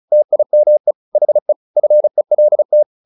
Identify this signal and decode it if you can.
Morse code, heard as a keyed tone, about 34 wpm, 600 Hz. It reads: TIME HE FELT
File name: the old man and the sea 2 35wpm